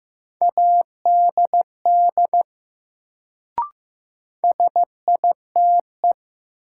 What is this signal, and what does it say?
Morse code, heard as a keyed tone, about 15 wpm, 700 Hz. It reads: ADD SITE
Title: Sentences from Top 1000 Words - Review Part 1 - New Word Drill - 15wpm